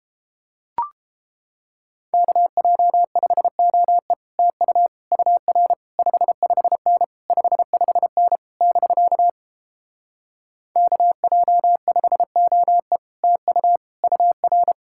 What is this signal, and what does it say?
Morse code, heard as a keyed tone, about 29 wpm, 700 Hz. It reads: KJ5OE TU UR 55N 55N <BK> KJ5OE TU UR